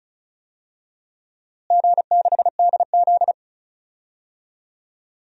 Morse code, heard as a keyed tone, about 36 wpm, 700 Hz. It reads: G6DZ
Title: DXCC-100 Call Signs - 4 Characters - Repeated 3x - 35wpm